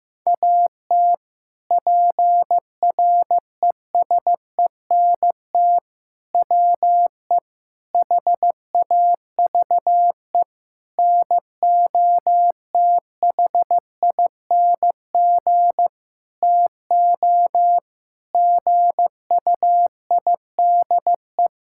Morse code, heard as a keyed tone, about 15 wpm, 700 Hz. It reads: AT PRESENT WE HAVE NOTHING TO GUIDE
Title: Sentences from Top 1000 Words - Review Part 5 - 15wpm